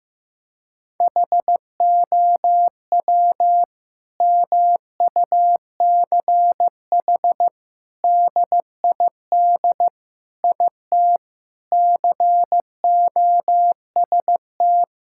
Morse code, heard as a keyed tone, about 15 wpm, 700 Hz. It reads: HOW MUCH DID IT COST
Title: Sentences from Top 1000 Words - Review Part 5 - 15wpm